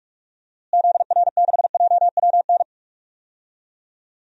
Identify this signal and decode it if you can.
Morse code, heard as a keyed tone, about 46 wpm, 700 Hz. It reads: ZR6JWN
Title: DXCC-100 Call Signs - 6 Characters - Repeated 2x - 45wpm